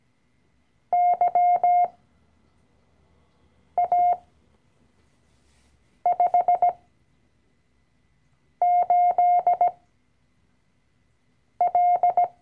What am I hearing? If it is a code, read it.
Morse code, heard as a keyed tone, about 17 wpm, spaced as for 4 wpm, 710 Hz. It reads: YA58L
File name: CW-Online-Kurs / Anfänger / Lektion 012